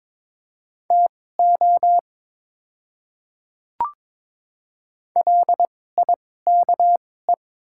Morse code, heard as a keyed tone, about 22 wpm, spaced as for 16 wpm, 700 Hz. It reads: TO LIKE
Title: 2x Character Spacing - Top 200 Words - 22wpm